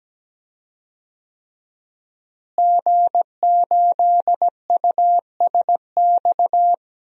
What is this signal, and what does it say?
Morse code, heard as a keyed tone, about 17 wpm, 700 Hz. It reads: G8USX
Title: DXCC-100 Call Signs - 5 Characters - Repeated 3x - 17wpm